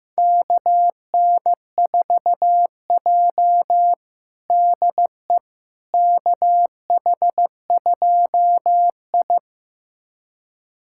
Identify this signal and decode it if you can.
Morse code, heard as a keyed tone, about 15 wpm, 700 Hz. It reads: KN4J DE KH2I
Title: USA Callsign DE Callsign - 4 Characters - 15wpm